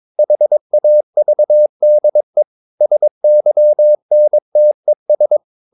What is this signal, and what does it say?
Morse code, heard as a keyed tone, about 22 wpm, 600 Hz. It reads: HAVDE SYNTES
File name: Ferie QRS 21wpm